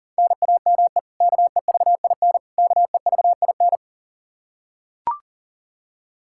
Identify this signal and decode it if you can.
Morse code, heard as a keyed tone, about 40 wpm, 700 Hz. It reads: NAME KEVIN KEVIN
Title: Practice QSO - Exchange 1 - Beginning - 40wpm